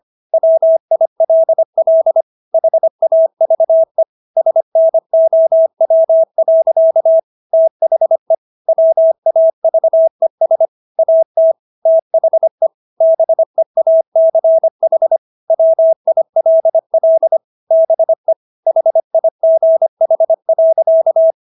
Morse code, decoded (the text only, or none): WILL HAVE SNOW. THE WAVES AT THE BEACH WILL BE HIGH.